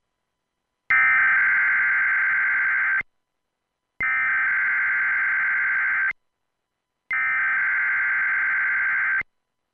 Emergency Alert System (SAME) tones (header bursts), heard as harsh buzzing bursts. It is heard from 0.9 to 9.2 s.